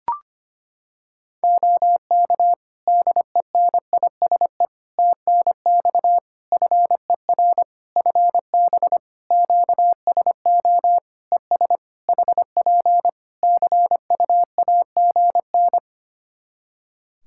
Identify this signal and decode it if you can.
Morse code, heard as a keyed tone, about 25 wpm, 700 Hz. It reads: OK DENISE TNX FER FB QSO ES HP CUAGN
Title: Practice QSO - Exchange 4 - Ending - 25wpm